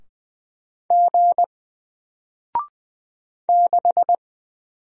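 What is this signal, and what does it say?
Morse code, heard as a keyed tone, about 20 wpm, 700 Hz. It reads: G 6
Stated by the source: Single letter and number Morse drill at 20 wpm, from N4LCD.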